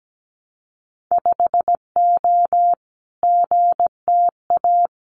Morse code, heard as a keyed tone, about 17 wpm, 700 Hz. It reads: 5O GTA
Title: Winter Field Day Exchanges Encore 17wpm